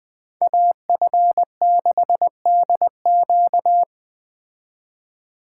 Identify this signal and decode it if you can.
Morse code, heard as a keyed tone, about 20 wpm, 700 Hz. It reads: AF6DQ